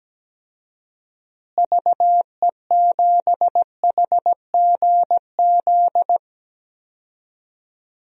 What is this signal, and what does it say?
Morse code, heard as a keyed tone, about 17 wpm, 700 Hz. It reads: VE7HGZ
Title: DXCC-100 Call Signs - 6 Characters - Repeated 2x - 17wpm